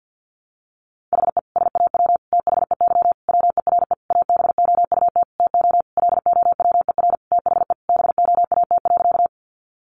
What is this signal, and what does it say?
Morse code, heard as a keyed tone, about 50 wpm, 700 Hz. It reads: HE SAW THEY WERE ABOUT TO LOWER THE BOAT.
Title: Sentences from Top 700 Words 50wpm